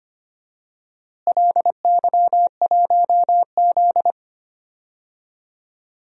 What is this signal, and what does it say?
Morse code, heard as a keyed tone, about 25 wpm, 700 Hz. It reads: LY1Z
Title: DXCC-100 Call Signs - 4 Characters - Repeated 2x - 25wpm